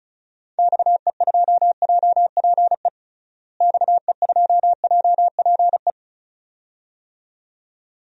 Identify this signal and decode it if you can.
Morse code, heard as a keyed tone, about 34 wpm, 700 Hz. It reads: XE2JPE XE2JPE
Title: DXCC-100 Call Signs - 6 Characters - Repeated 2x - 35wpm